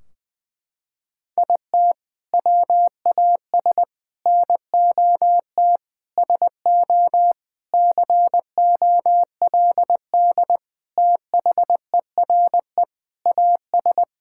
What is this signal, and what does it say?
Morse code, heard as a keyed tone, about 20 wpm, 700 Hz. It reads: IT WAS NOT SO COLD THERE AS
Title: Sentences from Top 1600 Words 20wpm